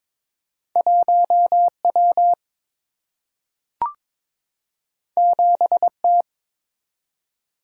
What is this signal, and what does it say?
Morse code, heard as a keyed tone, about 22 wpm, 700 Hz. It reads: 1 W 7 T
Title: Single-Letter Single-Number 22wpm